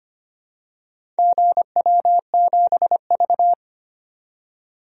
Morse code, heard as a keyed tone, about 25 wpm, 700 Hz. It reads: GW7V
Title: DXCC-100 Call Signs - 4 Characters - 25wpm